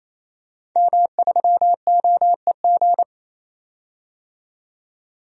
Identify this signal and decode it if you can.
Morse code, heard as a keyed tone, about 28 wpm, 700 Hz. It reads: M3OEG
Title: DXCC-100 Call Signs - 5 Characters - 28wpm